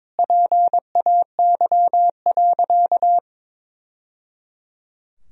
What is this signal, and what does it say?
Morse code, heard as a keyed tone, about 22 wpm, 700 Hz. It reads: PAY.
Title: Sentences from Top 1000 Words - Review Part 4 - 22wpm